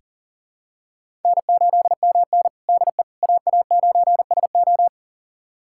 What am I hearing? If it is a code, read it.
Morse code, heard as a keyed tone, about 40 wpm, 700 Hz. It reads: N8MN DE AA9SO
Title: USA Callsign DE Callsign 40wpm